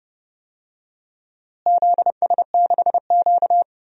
Morse code, heard as a keyed tone, about 30 wpm, 700 Hz. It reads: ZS6Q
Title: DXCC-100 Call Signs - 4 Characters - Repeated 2x - 30wpm